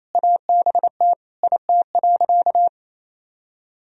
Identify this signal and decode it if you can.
Morse code, heard as a keyed tone, about 28 wpm, 700 Hz. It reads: ABT IT.
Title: Sentences from Top 500 Words and Abbreviations 28wpm